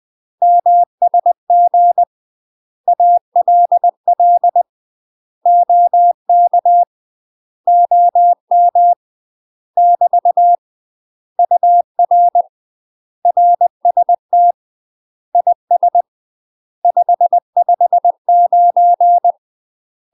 Morse code, heard as a keyed tone, about 20 wpm, 700 Hz. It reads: MSG ALL OK OM <BT> UR RST IS 559
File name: UK MASTER QSOs TRAINING / 20WPM / UK MASTER QSOs 0010